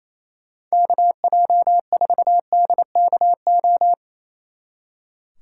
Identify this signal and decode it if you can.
Morse code, heard as a keyed tone, about 34 wpm, 700 Hz. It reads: KJ4DKO
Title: Speed Racing - USA 2x3 Call Signs 28wpm